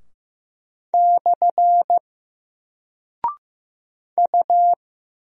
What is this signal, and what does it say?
Morse code, heard as a keyed tone, about 15 wpm, 700 Hz. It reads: / U
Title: Single Character (Letter:Number:Punctuation:Prosign) 15wpm